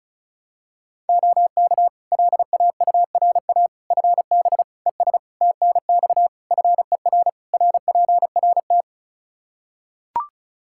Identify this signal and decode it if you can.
Morse code, heard as a keyed tone, about 36 wpm, 700 Hz. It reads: OK LAURA FB ES TNX FER RPRT
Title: K1USN SST QSO Practice 35wpm